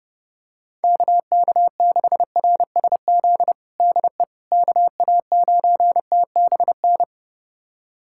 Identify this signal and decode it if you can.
Morse code, heard as a keyed tone, about 30 wpm, 700 Hz. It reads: KK6RSZ DE KA9TBN
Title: USA Callsign DE Callsign 30wpm